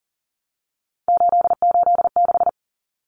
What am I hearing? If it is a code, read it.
Morse code, heard as a keyed tone, about 40 wpm, 700 Hz. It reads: 886
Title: Numbers - 3-Digit 40wpm